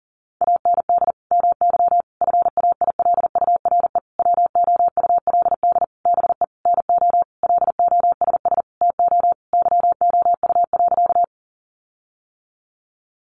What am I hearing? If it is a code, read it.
Morse code, heard as a keyed tone, about 40 wpm, 700 Hz. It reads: AND MY FAILURE WOULD BE NO LOSS TO YOU.
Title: Sentences from Top 1400 Words 40wpm